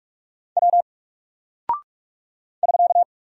Morse code, heard as a keyed tone, about 44 wpm, 700 Hz. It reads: W <SK>
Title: Single Character (Letter:Number:Punctuation:Prosign) 45wpm